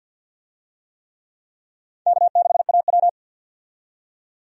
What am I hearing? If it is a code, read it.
Morse code, heard as a keyed tone, about 50 wpm, 700 Hz. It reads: K6AW
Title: USA Call Signs - 4 Characters - 50wpm